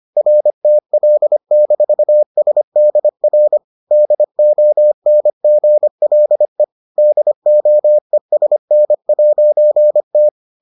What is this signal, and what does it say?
Morse code, heard as a keyed tone, about 25 wpm, 600 Hz. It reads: RTL-SDR DONGLE DOESN'T